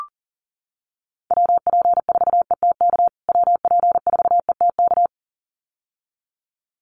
Morse code, heard as a keyed tone, about 40 wpm, 700 Hz. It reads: WP4ETK WP4ETK
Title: USA Call Signs Repeated 2x 40wpm